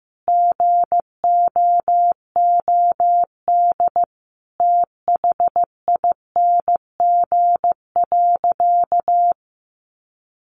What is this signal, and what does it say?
Morse code, heard as a keyed tone, about 15 wpm, 700 Hz. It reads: GOOD THING.
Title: Sentences from Top 1500 Words 15wpm